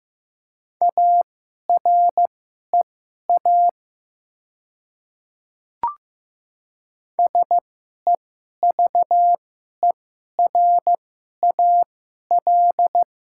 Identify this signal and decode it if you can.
Morse code, heard as a keyed tone, about 15 wpm, spaced as for 11 wpm, 700 Hz. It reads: AREA SEVERAL
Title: 2x Character Spacing - Top 500 Words - 15wpm